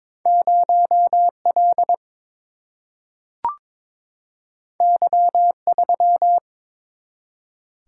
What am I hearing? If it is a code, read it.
Morse code, heard as a keyed tone, about 22 wpm, 700 Hz. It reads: 0 L Y 3
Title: Single-Letter Single-Number 22wpm